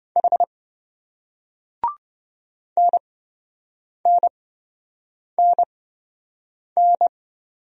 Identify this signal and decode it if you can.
Morse code, heard as a keyed tone, about 27 wpm, spaced as for 7 wpm, 700 Hz. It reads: H NNNN